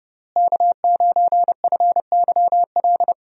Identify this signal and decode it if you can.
Morse code, heard as a keyed tone, about 31 wpm, 700 Hz. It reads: K9FYL
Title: USA Call Signs - 5 Characters - Repeated 2x - 30wpm